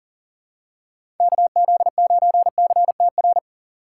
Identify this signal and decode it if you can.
Morse code, heard as a keyed tone, about 38 wpm, 700 Hz. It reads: KZ9CTR
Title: USA Call Signs - 3x Character Spacing - 40wpm